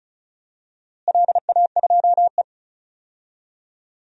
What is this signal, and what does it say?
Morse code, heard as a keyed tone, about 36 wpm, 700 Hz. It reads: LA2E
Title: DXCC-100 Call Signs - 4 Characters - Repeated 3x - 35wpm